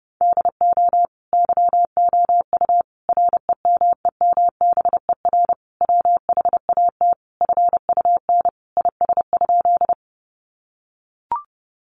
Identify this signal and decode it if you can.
Morse code, heard as a keyed tone, about 28 wpm, 700 Hz. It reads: DO YOU REMEMBER WHAT FUN IS?
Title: Sentences from Top 700 Words 30wpm